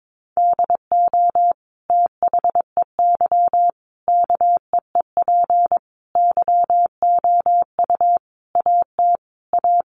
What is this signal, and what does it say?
Morse code, heard as a keyed tone, about 22 wpm, 700 Hz. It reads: DO THEY KEEP YOU AT A
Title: Sentences from Top 1200 Words 22wpm